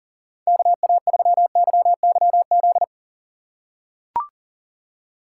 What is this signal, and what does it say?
Morse code, heard as a keyed tone, about 40 wpm, 700 Hz. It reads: KA3YYZ
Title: USA Call Signs - 3x Character Spacing - 40wpm